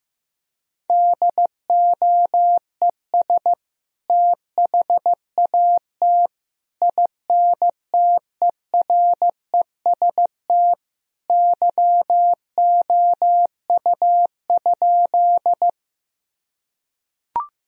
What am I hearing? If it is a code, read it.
Morse code, heard as a keyed tone, about 16 wpm, 700 Hz. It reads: DOES THAT INTEREST YOU?
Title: Sentences from Top 1000 Words - Review Part 2 - 15wpm